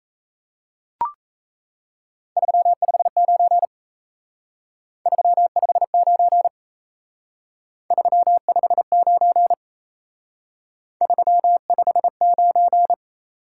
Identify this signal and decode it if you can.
Morse code, heard as a keyed tone, about 36 wpm, 700 Hz. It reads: 359 359 359 359